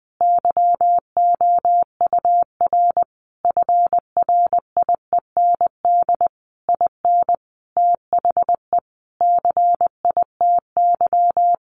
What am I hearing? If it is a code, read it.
Morse code, heard as a keyed tone, about 20 wpm, 700 Hz. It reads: YOUR FRIEND IN THE CITY
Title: Sentences from Top 1100 Words 20wpm